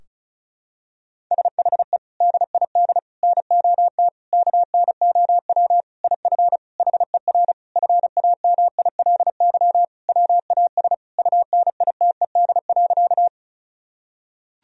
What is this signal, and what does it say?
Morse code, heard as a keyed tone, about 34 wpm, 700 Hz. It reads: SHE DID NOT KNOW IF HER FAMILY WAS UNITED.
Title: Sentences from Top 1000 Words - Review Part 4 - 35wpm